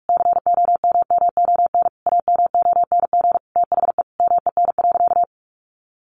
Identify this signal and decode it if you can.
Morse code, heard as a keyed tone, about 44 wpm, 700 Hz. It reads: COMMON AMONG THE MEN.
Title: Sentences from Top 1400 Words 45wpm